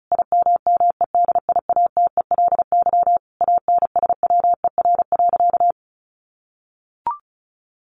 Morse code, heard as a keyed tone, about 34 wpm, 700 Hz. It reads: IMMEDIATELY ANSWER.
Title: Sentences from Top 1100 Words 35wpm